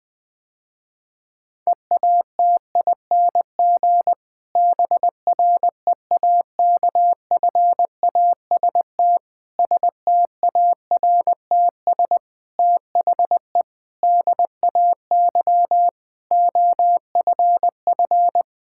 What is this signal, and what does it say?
Morse code, heard as a keyed tone, about 20 wpm, 700 Hz. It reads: EATING BREAKFAST STARTS THE DAY OFF
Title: Difficult Sentences with Hints - 20wpm